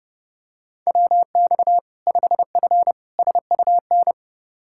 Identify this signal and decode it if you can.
Morse code, heard as a keyed tone, about 30 wpm, 700 Hz. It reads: WX 5F SUN
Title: Practice QSO - Exchange 2 - Info - 30wpm